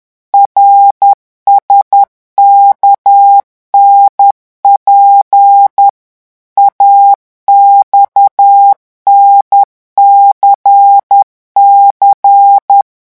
Morse code, heard as a keyed tone, about 10 wpm, 800 Hz. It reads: RSKNP AXNCC
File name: morse mp3 / revision5-vitesse 10 mots